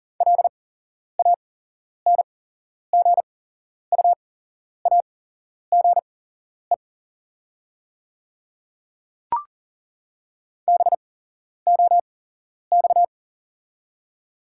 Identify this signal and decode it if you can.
Morse code, heard as a keyed tone, about 40 wpm, spaced as for 11 wpm, 690 Hz. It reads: LANGUAGE BOX
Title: 8x Character Spacing - Top 500 Words - 40wpm